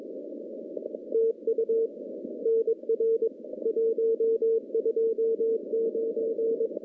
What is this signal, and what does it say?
Morse code, heard as a keyed tone, about 22 wpm, 450 Hz. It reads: TU NR 129
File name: contests / CW-SS-k5tr- -1300